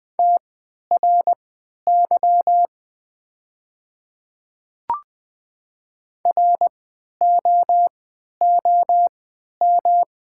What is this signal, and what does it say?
Morse code, heard as a keyed tone, about 20 wpm, spaced as for 11 wpm, 700 Hz. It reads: TRY ROOM
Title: 3x Character Spacing - Top 500 Words - 20wpm